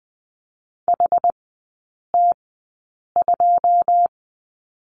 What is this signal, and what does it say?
Morse code, heard as a keyed tone, about 20 wpm, spaced as for 8 wpm, 700 Hz. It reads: HT2
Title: 100 Lesson 17e - Letter u - Three Characters 20wpm